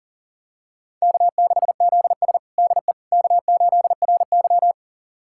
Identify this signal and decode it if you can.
Morse code, heard as a keyed tone, about 40 wpm, 700 Hz. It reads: K6ZS DE K8RY
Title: USA Callsign DE Callsign - 4 Characters - 40wpm